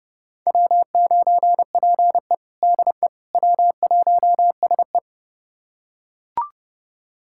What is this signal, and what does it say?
Morse code, heard as a keyed tone, about 31 wpm, 700 Hz. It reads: W9PE DE W1SE
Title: USA Callsign DE Callsign - 4 Characters - 30wpm